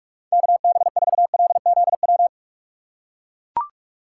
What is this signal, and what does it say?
Morse code, heard as a keyed tone, about 44 wpm, 700 Hz. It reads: KB4LZW
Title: USA Call Signs - 6 Characters - 6x Character Spacing - 45wpm